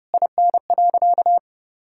Morse code, heard as a keyed tone, about 30 wpm, 700 Hz. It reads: IN.
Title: Sentences from Top 1000 Words - Review Part 5 - 30wpm